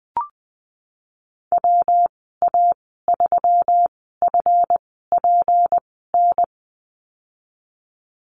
Morse code, heard as a keyed tone, about 19 wpm, spaced as for 14 wpm, 700 Hz. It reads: WA3FPN
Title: 167 Lesson 27g - Number 7 - Call Signs 20wpm